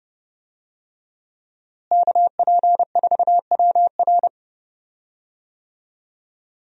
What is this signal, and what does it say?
Morse code, heard as a keyed tone, about 30 wpm, 700 Hz. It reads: KP4WR
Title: USA Call Signs - 5 Characters - Repeated 2x - 30wpm